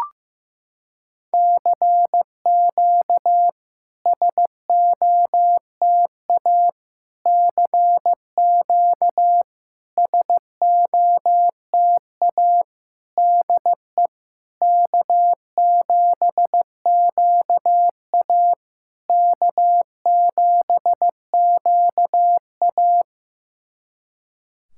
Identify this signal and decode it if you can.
Morse code, heard as a keyed tone, about 15 wpm, 700 Hz. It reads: CQ SOTA CQ SOTA DE K7QA K7QA
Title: SOTA QSO Practice 15wpm